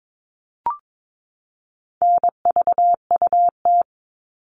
Morse code, heard as a keyed tone, about 22 wpm, 700 Hz. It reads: NVUT
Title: All Cut Numbers - 4-Digit 22wpm